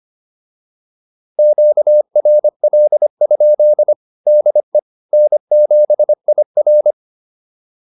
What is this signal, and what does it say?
Morse code, heard as a keyed tone, about 25 wpm, 600 Hz. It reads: QRL? DE N7IR
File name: QSO103 25